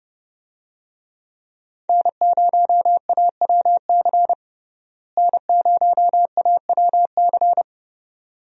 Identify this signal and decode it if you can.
Morse code, heard as a keyed tone, about 32 wpm, 700 Hz. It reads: N0AWC N0AWC
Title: USA Call Signs - 5 Characters - Repeated 2x - 30wpm